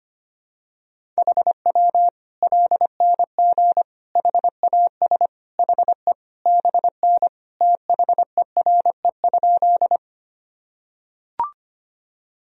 Morse code, heard as a keyed tone, about 25 wpm, 700 Hz. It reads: HW LNG HAS HE BN THERE?
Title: Sentences from Top 100 Words and Abbreviations 25wpm